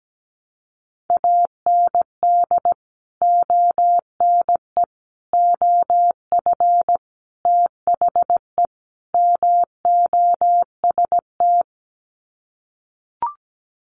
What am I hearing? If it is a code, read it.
Morse code, heard as a keyed tone, about 17 wpm, 700 Hz. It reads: AND ONE OF THE MOST
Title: Sets of 5 Words Encore 17wpm